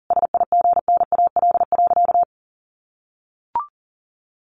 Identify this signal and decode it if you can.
Morse code, heard as a keyed tone, about 40 wpm, 700 Hz. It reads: SIGNAL.
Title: Sentences from Top 1400 Words 40wpm